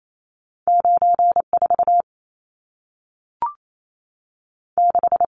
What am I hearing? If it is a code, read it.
Morse code, heard as a keyed tone, about 28 wpm, 700 Hz. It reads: 94 6
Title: Numbers - 2-Digit 28wpm